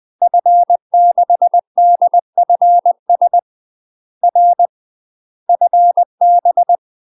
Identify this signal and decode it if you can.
Morse code, heard as a keyed tone, about 20 wpm, 700 Hz. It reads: F6DFS R FB
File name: UK MASTER QSOs TRAINING / 20WPM / UK MASTER QSOs 0020